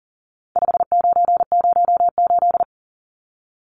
Morse code, heard as a keyed tone, about 40 wpm, 700 Hz. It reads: 5908
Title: Numbers - 4-Digit 40wpm